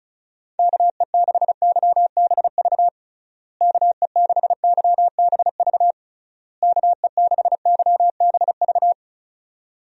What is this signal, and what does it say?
Morse code, heard as a keyed tone, about 36 wpm, 700 Hz. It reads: KE6YBV KE6YBV KE6YBV
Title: USA Call Signs - 6 Characters - Repeated 3x - 35wpm